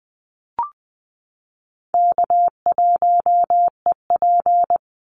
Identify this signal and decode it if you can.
Morse code, heard as a keyed tone, about 20 wpm, 700 Hz. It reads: K1EP
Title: Easy Call Signs Repeated 2x - 20wpm